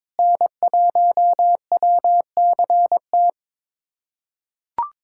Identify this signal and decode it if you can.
Morse code, heard as a keyed tone, about 22 wpm, 700 Hz. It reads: N1WCT
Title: USA Callsign DE Callsign - 5 Characters - 22wpm